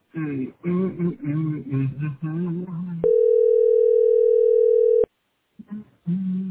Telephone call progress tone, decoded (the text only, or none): ringback tone